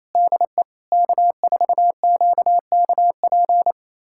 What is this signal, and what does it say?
Morse code, heard as a keyed tone, about 28 wpm, 700 Hz. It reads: DE K4QKP
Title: USA Callsign DE Callsign - 5 Characters - 28wpm